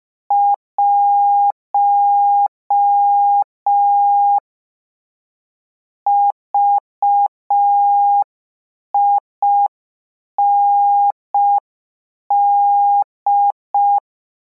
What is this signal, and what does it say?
Morse code, heard as a keyed tone, about 5 wpm, 810 Hz. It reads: 1 VIND